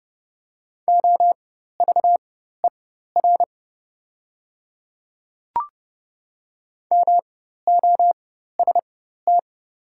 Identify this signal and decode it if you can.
Morse code, heard as a keyed tone, about 30 wpm, spaced as for 14 wpm, 700 Hz. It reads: OVER MOST
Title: 4x Character Spacing - Top 100 Words - 30wpm